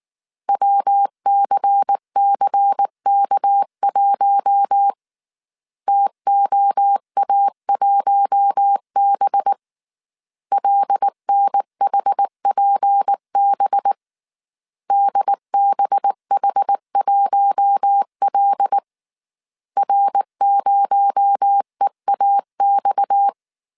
Morse code, heard as a keyed tone, about 19 wpm, 800 Hz. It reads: WCCK1 TOA1B LNHPB DBH1L R0EAX
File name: CW / DET / 1 6 1500C S A 18wpm 2255